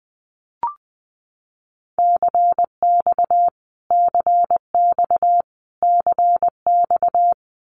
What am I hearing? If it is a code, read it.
Morse code, heard as a keyed tone, about 20 wpm, 700 Hz. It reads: CX CX CX
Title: Abbreviations Repeated 3x - 20wpm